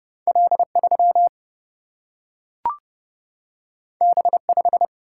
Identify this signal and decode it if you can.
Morse code, heard as a keyed tone, about 31 wpm, 700 Hz. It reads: L3 B5
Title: Single-Letter Single-Number 30wpm